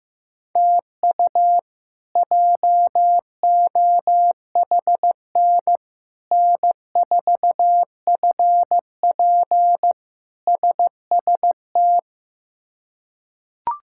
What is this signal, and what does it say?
Morse code, heard as a keyed tone, about 16 wpm, 700 Hz. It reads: TU JOHN N4FP SST
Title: K1USN SST QSO Practice - 15wpm